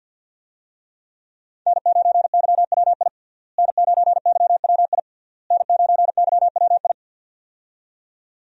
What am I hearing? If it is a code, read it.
Morse code, heard as a keyed tone, about 50 wpm, 700 Hz. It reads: N9YWI N9YWI N9YWI
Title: USA Call Signs - 5 Characters - Repeated 3x - 50wpm